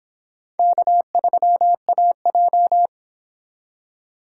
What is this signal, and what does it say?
Morse code, heard as a keyed tone, about 32 wpm, 700 Hz. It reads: K3AJ